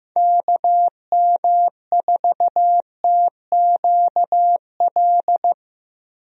Morse code, heard as a keyed tone, about 15 wpm, 700 Hz. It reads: KM4TQL
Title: USA Call Signs - 6 Characters - Repeated 3x - 15wpm